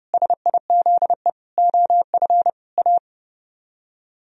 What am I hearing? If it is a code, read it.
Morse code, heard as a keyed tone, about 33 wpm, 700 Hz. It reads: SIZE OF A